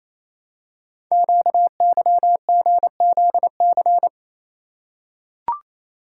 Morse code, heard as a keyed tone, about 29 wpm, 700 Hz. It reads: QYGZC
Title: Code Groups - 5 Letters - 28wpm